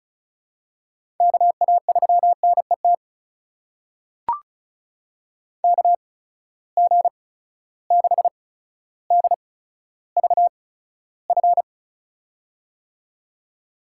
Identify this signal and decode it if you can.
Morse code, heard as a keyed tone, about 34 wpm, spaced as for 15 wpm, 700 Hz. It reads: KA3NET KG6DVF
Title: USA Call Signs - 6 Characters - 8x Character Spacing - 35wpm